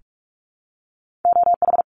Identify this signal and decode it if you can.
Morse code, heard as a keyed tone, about 44 wpm, 700 Hz. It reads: OH
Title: ARRL Sections 45wpm